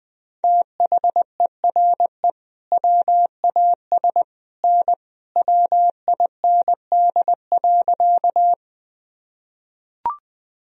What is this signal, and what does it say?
Morse code, heard as a keyed tone, about 19 wpm, 700 Hz. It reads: THERE WAS N WIND.
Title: Sentences from Top 400 Words and Abbreviations 20wpm